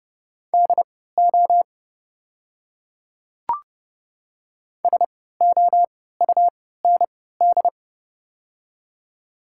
Morse code, heard as a keyed tone, about 30 wpm, spaced as for 17 wpm, 700 Hz. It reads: DO SOUND
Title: 3x Character Spacing - Top 100 Words - 30wpm